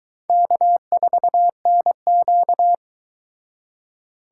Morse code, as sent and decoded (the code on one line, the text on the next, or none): -.- ....- -. --.-
K4NQ